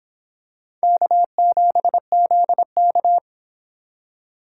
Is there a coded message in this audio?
Yes: Morse code from 0.8 to 3.2 s, heard as a keyed tone, about 26 wpm, 700 Hz. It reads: K7ZK